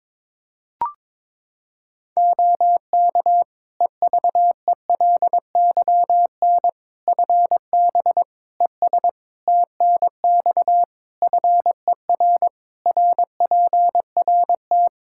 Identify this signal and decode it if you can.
Morse code, heard as a keyed tone, about 22 wpm, 700 Hz. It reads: OK EVELYN FB ES TNX FER RPRT
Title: Practice QSO - Exchange 2 - Info - 22wpm